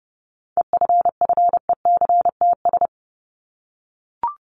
Morse code, heard as a keyed tone, about 30 wpm, 700 Hz. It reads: EFFECTS
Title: ES:S Suffix - Top 100 Words with Base Words 30wpm